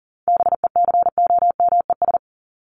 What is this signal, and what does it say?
Morse code, heard as a keyed tone, about 40 wpm, 700 Hz. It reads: BECOMES
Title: ES:S Suffix - Top 100 Words with Base Words 40wpm